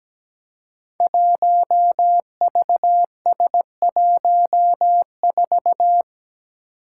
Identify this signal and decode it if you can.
Morse code, heard as a keyed tone, about 17 wpm, 700 Hz. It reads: 1VS14
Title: Five Challenging Letter-Number Combinations 17wpm